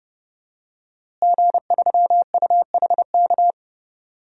Morse code, heard as a keyed tone, about 30 wpm, 700 Hz. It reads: G3UHK
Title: DXCC-100 Call Signs - 5 Characters - Repeated 3x - 30wpm